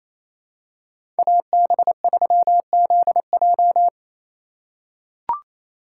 Morse code, heard as a keyed tone, about 28 wpm, 700 Hz. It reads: AB3ZJ
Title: USA Call Signs - 5 Characters - 2x Character Spacing - 28wpm